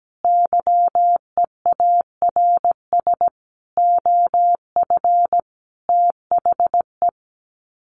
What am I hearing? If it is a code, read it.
Morse code, heard as a keyed tone, about 17 wpm, 700 Hz. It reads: YEARS OF THE
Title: Sets of 5 Words Encore 17wpm